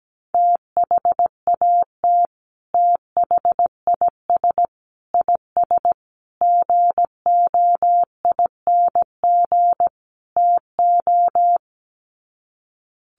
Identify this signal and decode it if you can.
Morse code, heard as a keyed tone, about 17 wpm, 700 Hz. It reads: THAT THIS IS GOING TO
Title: Sets of 5 Words Encore2 17wpm